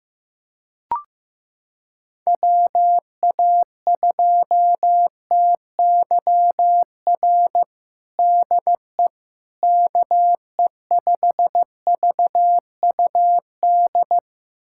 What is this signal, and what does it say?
Morse code, heard as a keyed tone, about 16 wpm, 700 Hz. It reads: WA2TYR DE KE5VUD
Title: USA Callsign DE Callsign - 6 Characters - 15wpm